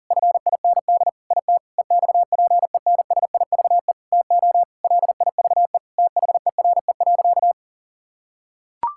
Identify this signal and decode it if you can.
Morse code, heard as a keyed tone, about 37 wpm, 690 Hz. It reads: FIND IT EXPENSIVE TO LIVE THERE.
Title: Sentences from Top 1000 Words - Review Part 6 - 40wpm